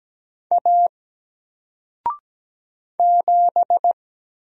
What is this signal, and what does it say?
Morse code, heard as a keyed tone, about 17 wpm, 700 Hz. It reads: A 7
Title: Single Letter-Number 17wpm